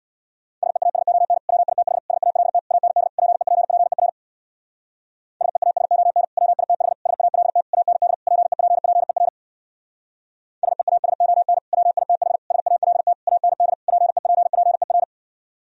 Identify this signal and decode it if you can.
Morse code, heard as a keyed tone, about 64 wpm, 700 Hz. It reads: SEASON WITH SALT AND PEPPER SEASON WITH SALT AND PEPPER SEASON WITH SALT AND PEPPER